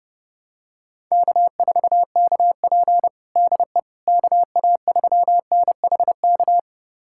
Morse code, heard as a keyed tone, about 30 wpm, 700 Hz. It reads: K4KP DE KA3NHK
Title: USA Callsign DE Callsign 30wpm